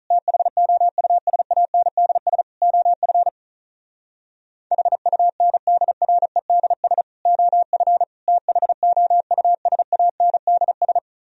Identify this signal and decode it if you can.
Morse code, heard as a keyed tone, about 47 wpm, 700 Hz. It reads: THOUSANDS OF HUNDREDS OF THOUSANDS